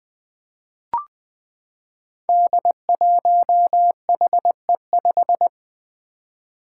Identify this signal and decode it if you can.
Morse code, heard as a keyed tone, about 20 wpm, 700 Hz. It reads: D1HE5
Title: Five Challenging Letter-Number Combinations 20wpm